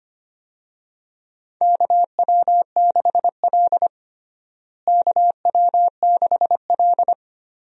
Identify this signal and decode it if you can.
Morse code, heard as a keyed tone, about 25 wpm, 700 Hz. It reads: KW6L KW6L
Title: USA Call Signs - 4 Characters - Repeated 2x - 25wpm